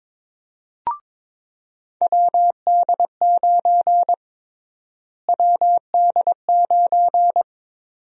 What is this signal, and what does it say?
Morse code, heard as a keyed tone, about 22 wpm, 700 Hz. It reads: WD9 WD9
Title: USA Call Sign Prefixes - Repeated 2x - 22wpm